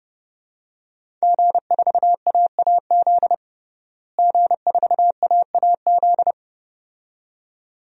Morse code, heard as a keyed tone, about 30 wpm, 700 Hz. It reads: G4AAZ G4AAZ
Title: DXCC-100 Call Signs - 5 Characters - Repeated 2x - 30wpm